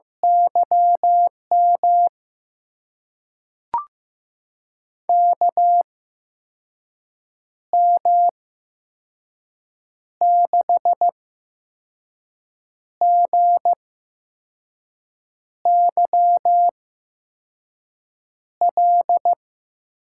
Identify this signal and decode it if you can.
Morse code, heard as a keyed tone, about 15 wpm, spaced as for 5 wpm, 700 Hz. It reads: YM KM6GYL
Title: USA Call Signs - 6 Characters - 8x Character Spacing - 15wpm